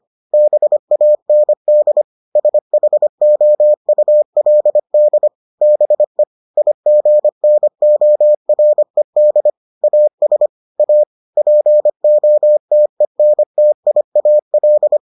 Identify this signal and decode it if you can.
Morse code, heard as a keyed tone, about 25 wpm, 600 Hz. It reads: BAND SHOULD BE IGNORED AS A POTENTIAL